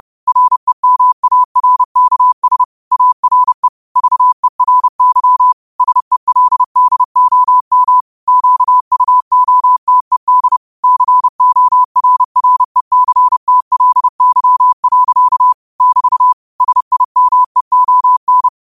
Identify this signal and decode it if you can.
Morse code, heard as a keyed tone, about 30 wpm, 1000 Hz. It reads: REMARKS ARE VERY SELDOM QUOTED CORRECTLY. <BT> SIMEON
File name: qotdfiles / 30wpm